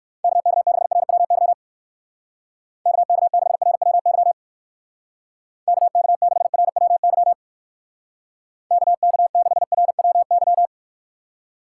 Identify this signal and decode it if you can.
Morse code, heard as a keyed tone, about 56 wpm, 700 Hz. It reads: KK6RWY KK6RWY KK6RWY KK6RWY